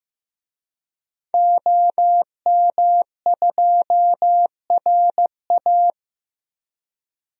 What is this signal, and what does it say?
Morse code, heard as a keyed tone, about 15 wpm, 700 Hz. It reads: OM2RA
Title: DXCC-100 Call Signs - 5 Characters - Repeated 2x - 15wpm